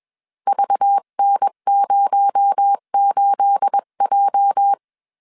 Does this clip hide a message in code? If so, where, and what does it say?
Yes: Morse code from 0.5 to 4.7 s, heard as a keyed tone, about 21 wpm, 800 Hz. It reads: VN08J